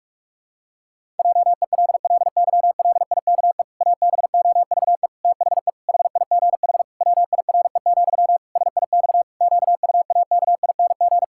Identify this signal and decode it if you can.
Morse code, heard as a keyed tone, about 46 wpm, 700 Hz. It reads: JELLYLIKE ABOVE THE HIGH WIRE, SIX QUAKING
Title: Pangram Challenge v2 - 45wpm